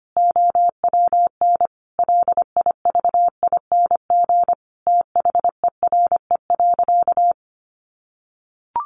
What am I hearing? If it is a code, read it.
Morse code, heard as a keyed tone, about 25 wpm, 700 Hz. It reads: OWN LIVING THERE.
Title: Sentences from Top 1500 Words 25wpm